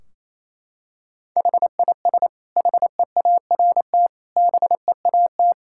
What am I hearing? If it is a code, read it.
Morse code, heard as a keyed tone, about 27 wpm, 700 Hz. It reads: HIS HEART BEAT
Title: Sentences from Top 1600 Words 28wpm